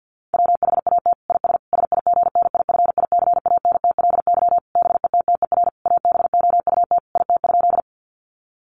Morse code, heard as a keyed tone, about 48 wpm, 700 Hz. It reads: WHAT IS SIGNIFICANTLY BETTER ABOUT IT?
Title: Sentences from Top 1500 Words 50wpm